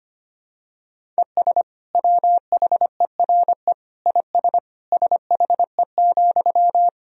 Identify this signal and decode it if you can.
Morse code, heard as a keyed tone, about 25 wpm, 700 Hz. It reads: ES WHERE IS SHE,
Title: Sentences from Top 200 Words and Abbreviations 25wpm